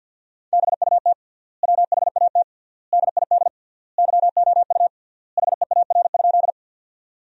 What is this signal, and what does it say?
Morse code, heard as a keyed tone, about 50 wpm, 700 Hz. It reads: BUT WHAT DID YOU HEAR?
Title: 3x Word-Spacing - Sentences from Top 300 Words - 50wpm